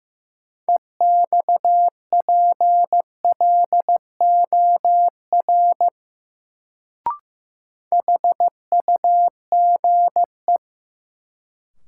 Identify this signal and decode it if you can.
Morse code, heard as a keyed tone, about 15 wpm, 700 Hz. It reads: EXPLOR HUGE
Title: Sentences from Top 1100 Words - New Word Drill - 15wpm